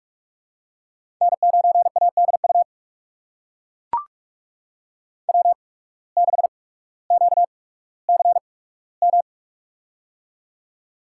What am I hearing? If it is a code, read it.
Morse code, heard as a keyed tone, about 44 wpm, spaced as for 19 wpm, 700 Hz. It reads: N9ADU W6QCM
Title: USA Call Signs - 5 Characters - 8x Character Spacing - 45wpm